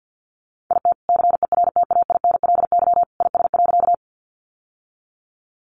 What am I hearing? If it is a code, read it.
Morse code, heard as a keyed tone, about 50 wpm, 700 Hz. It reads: IT CERTAINLY IS.
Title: Sentences from Top 900 Words 50wpm